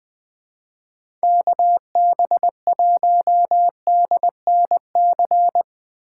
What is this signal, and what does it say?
Morse code, heard as a keyed tone, about 20 wpm, 700 Hz. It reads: KB1DNC
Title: USA Call Signs - 6 Characters - 6x Character Spacing - 20wpm